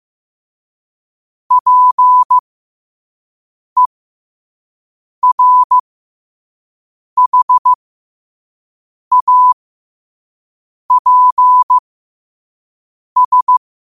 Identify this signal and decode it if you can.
Morse code, heard as a keyed tone, about 15 wpm, spaced as for 5 wpm, 1000 Hz. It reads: PERHAPS
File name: qotdfiles / 5wpm